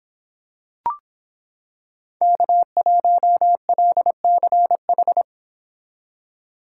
Morse code, heard as a keyed tone, about 26 wpm, 700 Hz. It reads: K1LCH